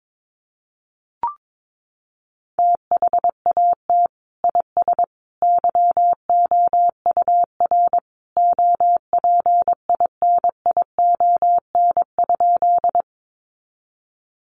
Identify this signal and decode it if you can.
Morse code, heard as a keyed tone, about 22 wpm, 700 Hz. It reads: THAT IS YOUR OPINION?
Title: Sentences from Top 1200 Words 22wpm